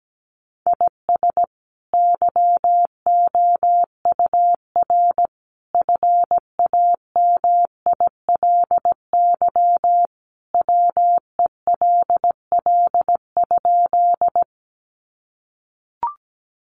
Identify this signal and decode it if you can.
Morse code, heard as a keyed tone, about 17 wpm, 700 Hz. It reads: IS YOUR FAMILY WELL?
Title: Sentences from Top 400 Words 17wpm